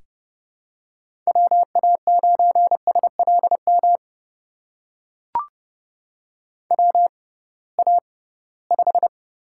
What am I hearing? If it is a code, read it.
Morse code, heard as a keyed tone, about 32 wpm, spaced as for 20 wpm, 700 Hz. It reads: WA9SLM WA5
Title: USA Call Signs - 6 Characters - 6x Character Spacing - 30wpm